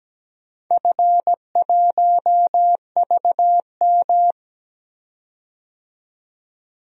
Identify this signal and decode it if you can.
Morse code, heard as a keyed tone, about 17 wpm, 700 Hz. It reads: F1VM
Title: DXCC-100 Call Signs - 4 Characters - 17wpm